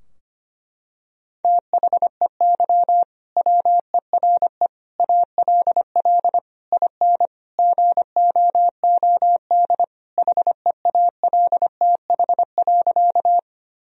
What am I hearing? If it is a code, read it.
Morse code, heard as a keyed tone, about 25 wpm, 700 Hz. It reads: THEY WERE ALL IN GOOD HEALTH.
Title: Sentences from Top 1000 Words - Review Part 3 - 25wpm